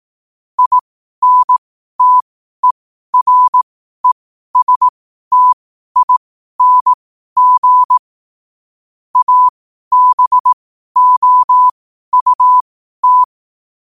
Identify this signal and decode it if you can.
Morse code, heard as a keyed tone, about 18 wpm, spaced as for 12 wpm, 1000 Hz. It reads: INTERESTING ABOUT
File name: qotdfiles / 12wpm